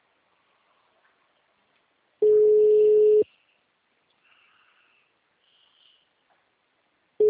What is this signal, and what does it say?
Ringback tone (Europe, 425 Hz).